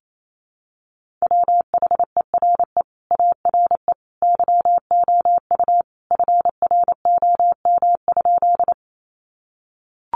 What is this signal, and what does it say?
Morse code, heard as a keyed tone, about 28 wpm, 700 Hz. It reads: WHERE ARE YOU FROM?
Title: Sentences from Top 200 Words 28wpm